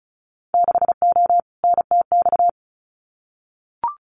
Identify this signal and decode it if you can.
Morse code, heard as a keyed tone, about 34 wpm, 700 Hz. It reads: 6O NTX
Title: Winter Field Day Exchanges Encore 35wpm